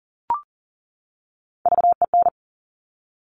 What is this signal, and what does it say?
Morse code, heard as a keyed tone, about 41 wpm, 690 Hz. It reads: VEN
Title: Top Three Letter Combinations in Words 40wpm